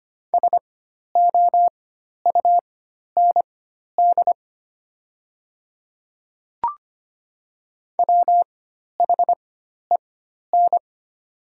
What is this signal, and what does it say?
Morse code, heard as a keyed tone, about 25 wpm, spaced as for 12 wpm, 700 Hz. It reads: SOUND WHEN
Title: 4x Character Spacing - Top 100 Words - 25wpm